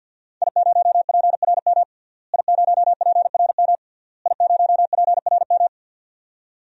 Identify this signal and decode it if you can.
Morse code, heard as a keyed tone, about 48 wpm, 700 Hz. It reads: I0PRM I0PRM I0PRM
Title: DXCC-100 Call Signs - 5 Characters - Repeated 3x - 50wpm